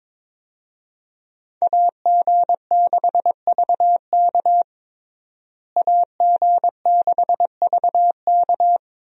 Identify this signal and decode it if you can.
Morse code, heard as a keyed tone, about 22 wpm, 700 Hz. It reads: AG6VK AG6VK
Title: USA Call Signs - 5 Characters - Repeated 3x - 22wpm